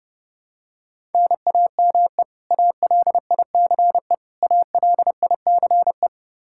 Morse code, heard as a keyed tone, about 30 wpm, 700 Hz. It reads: NAME ALICE ALICE
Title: Practice QSO - Exchange 1 - Beginning - 30wpm